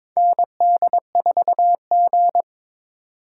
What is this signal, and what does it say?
Morse code, heard as a keyed tone, about 22 wpm, 700 Hz. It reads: ND4G
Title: K1USN SST QSO Practice - 22wpm